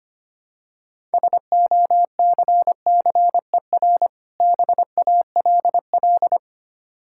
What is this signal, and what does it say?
Morse code, heard as a keyed tone, about 25 wpm, 700 Hz. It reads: SOCCER BALL
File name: practice / 25wpm / Difficult Words with Hints - 25wpm